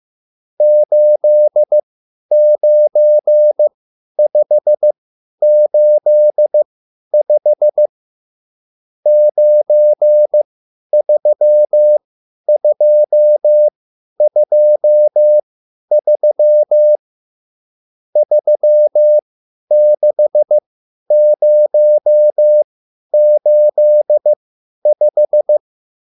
Morse code, heard as a keyed tone, about 15 wpm, spaced as for 10 wpm, 600 Hz. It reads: 89585 93223 36085